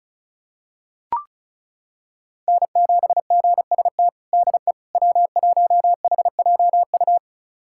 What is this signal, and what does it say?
Morse code, heard as a keyed tone, about 36 wpm, 700 Hz. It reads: N7GST DE W1HJU
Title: USA Callsign DE Callsign - 5 Characters - 35wpm